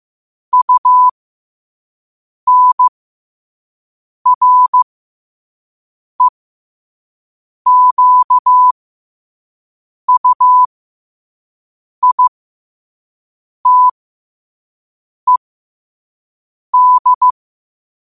Morse code, heard as a keyed tone, about 15 wpm, spaced as for 5 wpm, 1000 Hz. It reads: UNREQUITED
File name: 5wpm